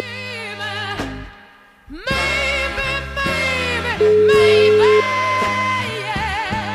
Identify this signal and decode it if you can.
Ringback tone (Europe, 425 Hz), with noise.